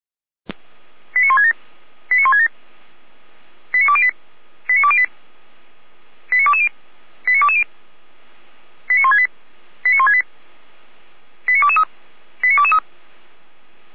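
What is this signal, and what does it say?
Five-tone selective call (ZVEI), heard as quick runs of five tones, tones of about 70 ms. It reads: 89167 89167 89208 89208 89220 89220 89168 89168 89223 89223
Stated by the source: ZVEI five-tone call received by SDR, fire brigade Feuerwehr Orlamünde.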